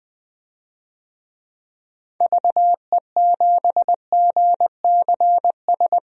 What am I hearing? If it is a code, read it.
Morse code, heard as a keyed tone, about 20 wpm, 700 Hz. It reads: VE7GCS
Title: DXCC-100 Call Signs - 6 Characters - 20wpm